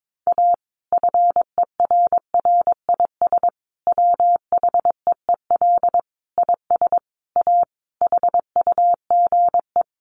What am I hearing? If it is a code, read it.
Morse code, heard as a keyed tone, about 22 wpm, 700 Hz. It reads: A FERRIS WHEEL IS A HUGE
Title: Additional Easy Phrases 22wpm